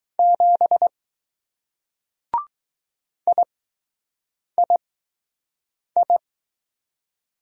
Morse code, heard as a keyed tone, about 19 wpm, 700 Hz. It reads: 7 I I I